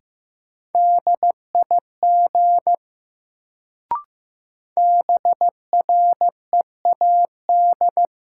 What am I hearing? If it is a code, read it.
Morse code, heard as a keyed tone, about 15 wpm, 700 Hz. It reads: DIG BREAD
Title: Sentences from Top 1600 Words - New Word Drill Alt - 15wpm